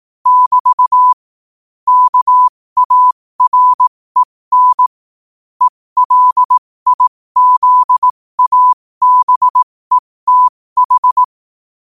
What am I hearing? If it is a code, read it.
Morse code, heard as a keyed tone, about 18 wpm, spaced as for 15 wpm, 1000 Hz. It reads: <BT> KAREN ELIZABETH
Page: New quotes every day in morse code at 15 Words per minute.